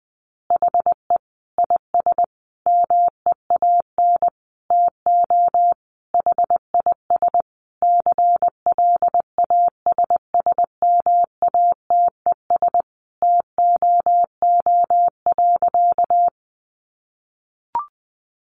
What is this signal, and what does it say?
Morse code, heard as a keyed tone, about 20 wpm, 700 Hz. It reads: HE IS MEAN TO HIS CLASSMATES TOO.
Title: Additional Easy Phrases 20wpm